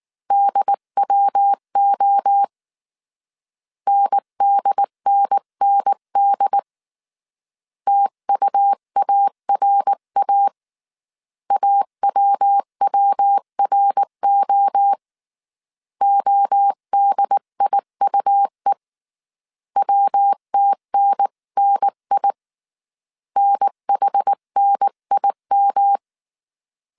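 Morse code, heard as a keyed tone, about 19 wpm, 800 Hz. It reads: DWO NDNND TUARA AWWRO ODIUE WTNNI NHNIM
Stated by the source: Morse code dictation at 18 wpm from the IW3SGT QRS Morse code training course, five-character groups of 300 characters.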